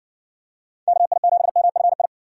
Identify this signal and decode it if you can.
Morse code, heard as a keyed tone, about 60 wpm, 700 Hz. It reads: KI7MFI